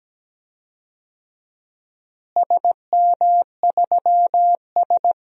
Morse code, heard as a keyed tone, about 17 wpm, 700 Hz. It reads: SM3S
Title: DXCC-100 Call Signs - 4 Characters - Repeated 3x - 17wpm